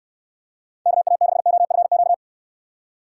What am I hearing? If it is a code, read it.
Morse code, heard as a keyed tone, about 67 wpm, 700 Hz. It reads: KA6OFQ